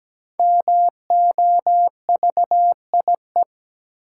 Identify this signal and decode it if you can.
Morse code, heard as a keyed tone, about 17 wpm, 700 Hz. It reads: MOVIE
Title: Five Letter Words - 17wpm